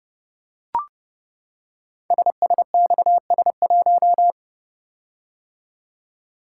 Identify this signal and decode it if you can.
Morse code, heard as a keyed tone, about 30 wpm, 690 Hz. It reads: SSXS1
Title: Code Groups - 5 Letter:Numbers - 30wpm